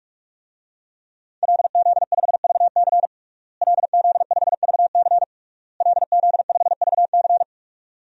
Morse code, heard as a keyed tone, about 46 wpm, 700 Hz. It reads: LZ5VC LZ5VC LZ5VC
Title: DXCC-100 Call Signs - 5 Characters - Repeated 3x - 45wpm